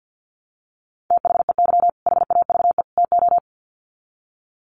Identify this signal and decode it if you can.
Morse code, heard as a keyed tone, about 49 wpm, 700 Hz. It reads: THEY HAVE TO
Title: Sets of 3 Words 50wpm